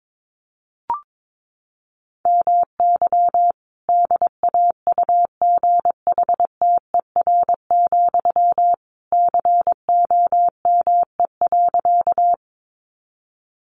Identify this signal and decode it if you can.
Morse code, heard as a keyed tone, about 22 wpm, 700 Hz. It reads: MY DAUGHTER, COME.
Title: Sentences from Top 1500 Words 22wpm